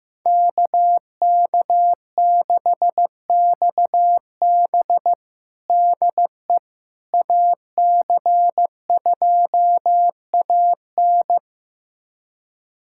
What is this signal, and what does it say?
Morse code, heard as a keyed tone, about 15 wpm, 700 Hz. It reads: KK6XB DE AC2AN
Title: USA Callsign DE Callsign - 5 Characters - 15wpm